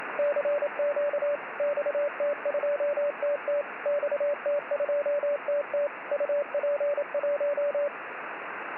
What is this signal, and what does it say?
Morse code, heard as a keyed tone, about 27 wpm, 580 Hz, with radio noise. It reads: CQ XT2TT XT2TT UP1
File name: XT2TT CW